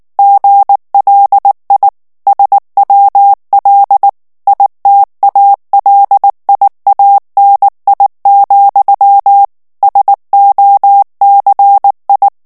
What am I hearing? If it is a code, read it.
Morse code, heard as a keyed tone, about 19 wpm, 800 Hz. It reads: GLI SWL ITALIANI, SOCI
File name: morse mp3 / ITALIEN texte clair 2 vitesse 18 mots minute